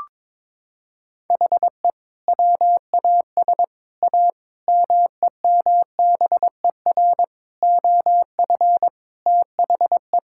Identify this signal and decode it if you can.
Morse code, heard as a keyed tone, about 22 wpm, 700 Hz. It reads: HE WAS A MEMBER OF THE
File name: practice / 22wpm / Sets of 6 Words - 22wpm